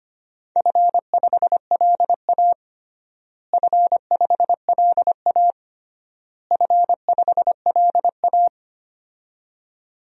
Morse code, heard as a keyed tone, about 25 wpm, 700 Hz. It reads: F5LA F5LA F5LA
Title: DXCC-100 Call Signs - 4 Characters - Repeated 3x - 25wpm